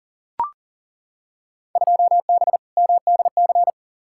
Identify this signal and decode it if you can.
Morse code, heard as a keyed tone, about 40 wpm, 700 Hz. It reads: 2B MDC
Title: Field Day Exchanges - 40wpm